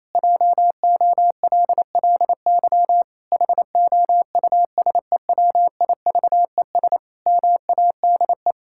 Morse code, heard as a keyed tone, about 28 wpm, 700 Hz. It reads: JOLLY HOUSEWIVES MADE
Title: Pangram Challenge v2 - 28wpm